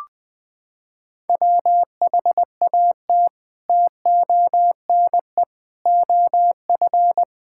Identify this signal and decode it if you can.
Morse code, heard as a keyed tone, about 21 wpm, 700 Hz. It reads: WHAT TONE OF
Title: Sentences from Top 1600 Words 20wpm